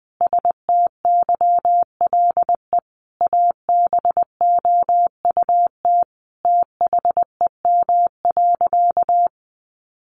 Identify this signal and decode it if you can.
Morse code, heard as a keyed tone, about 20 wpm, 700 Hz. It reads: STYLE ABOUT THEM.
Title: Sentences from Top 800 Words 20wpm